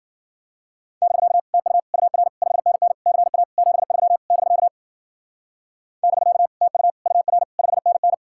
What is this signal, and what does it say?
Morse code, heard as a keyed tone, about 52 wpm, 700 Hz. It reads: <BK> TU UR 5NN CA 73 <BK> <BK> TU UR 5NN